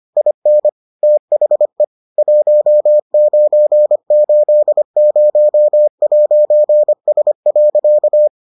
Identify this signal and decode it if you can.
Morse code, heard as a keyed tone, about 25 wpm, 600 Hz. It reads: IN THE 1980'S.